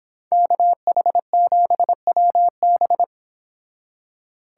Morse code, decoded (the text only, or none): KH7WB